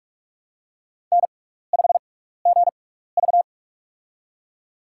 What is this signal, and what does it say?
Morse code, heard as a keyed tone, about 43 wpm, spaced as for 15 wpm, 690 Hz. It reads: N5GV
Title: USA Call Signs - 6x Character Spacing - 45wpm